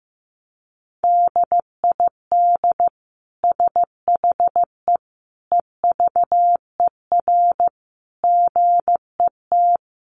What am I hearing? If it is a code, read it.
Morse code, heard as a keyed tone, about 15 wpm, 700 Hz. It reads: DID SHE EVER GET
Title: Sentences from Top 1100 Words 15wpm